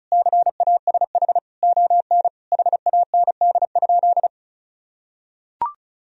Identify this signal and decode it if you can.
Morse code, heard as a keyed tone, about 34 wpm, 700 Hz. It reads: CASH ON HAND?
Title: Sentences from Top 1000 Words - Review Part 4 - 35wpm